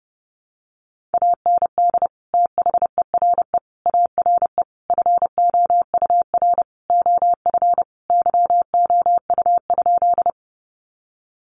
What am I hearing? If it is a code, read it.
Morse code, heard as a keyed tone, about 30 wpm, 700 Hz. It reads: AND THERE ARE FOUR OF YOU?
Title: Sentences from Top 300 Words 30wpm